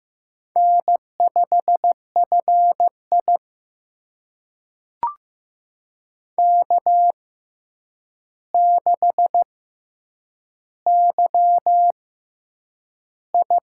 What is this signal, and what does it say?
Morse code, heard as a keyed tone, about 15 wpm, spaced as for 8 wpm, 700 Hz. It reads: N5FI K6YI
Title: USA Call Signs - 4 Characters - 6x Character Spacing - 15wpm